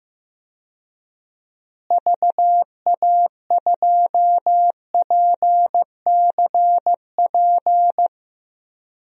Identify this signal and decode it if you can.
Morse code, heard as a keyed tone, about 15 wpm, 700 Hz. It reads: VA2PCP
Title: DXCC-100 Call Signs - 6 Characters - Repeated 3x - 15wpm